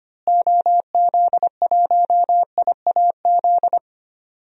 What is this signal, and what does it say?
Morse code, heard as a keyed tone, about 25 wpm, 700 Hz. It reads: OZ1IAZ
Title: DXCC-100 Call Signs - 6 Characters - Repeated 2x - 25wpm